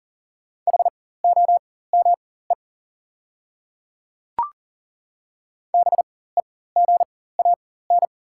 Morse code, heard as a keyed tone, about 38 wpm, spaced as for 19 wpm, 690 Hz. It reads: HOME BEGAN
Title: 4x Character Spacing - Top 300 Words - 40wpm